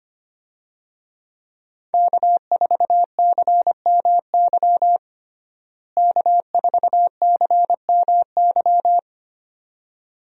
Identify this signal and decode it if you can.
Morse code, heard as a keyed tone, about 25 wpm, 700 Hz. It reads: K4CMY K4CMY
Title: USA Call Signs - 5 Characters - Repeated 2x - 25wpm